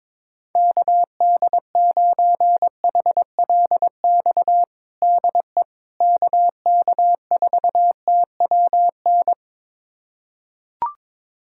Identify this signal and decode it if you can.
Morse code, heard as a keyed tone, about 22 wpm, 700 Hz. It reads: KD9HLX DE KK4TWN
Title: USA Callsign DE Callsign - 6 Characters - 22wpm